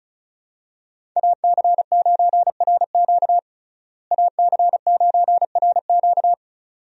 Morse code, heard as a keyed tone, about 36 wpm, 700 Hz. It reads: AC9RQ AC9RQ
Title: USA Call Signs - 5 Characters - Repeated 2x - 35wpm